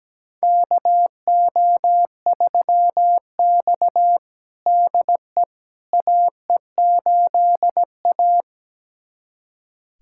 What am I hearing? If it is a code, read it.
Morse code, heard as a keyed tone, about 17 wpm, 700 Hz. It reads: KO3X DE AE8A
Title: USA Callsign DE Callsign - 4 Characters - 17wpm